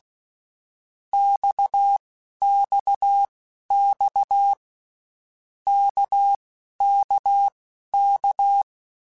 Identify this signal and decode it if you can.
Morse code, heard as a keyed tone, about 16 wpm, spaced as for 12 wpm, 800 Hz. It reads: XXX KKK
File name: Basic Letter Groups By7Groups / RKPX / RKPX@10